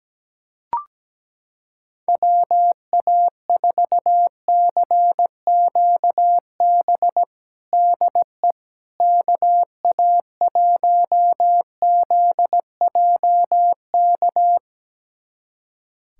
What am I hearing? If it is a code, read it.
Morse code, heard as a keyed tone, about 17 wpm, 700 Hz. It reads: WA4CQB DE KA1ZJK
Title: USA Callsign DE Callsign - 6 Characters - 17wpm